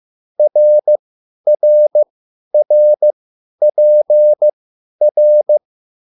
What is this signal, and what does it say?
Morse code, heard as a keyed tone, about 15 wpm, spaced as for 10 wpm, 600 Hz. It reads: RRRPR
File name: QSL / CW lesson